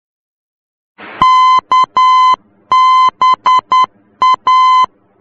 Morse code, heard as a keyed tone, about 10 wpm, 1020 Hz. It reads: KBA
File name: germany / mx / KBA-D